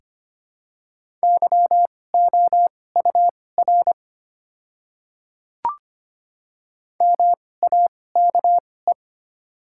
Morse code, heard as a keyed tone, about 25 wpm, spaced as for 18 wpm, 690 Hz. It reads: YOUR MAKE
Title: 2x Character Spacing - Top 100 Words - 25wpm